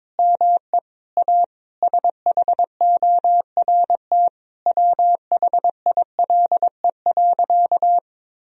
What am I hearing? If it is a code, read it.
Morse code, heard as a keyed tone, about 22 wpm, 700 Hz. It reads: ME A SHORT WHILE.
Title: Sentences from Top 1000 Words - Review Part 2 - 22wpm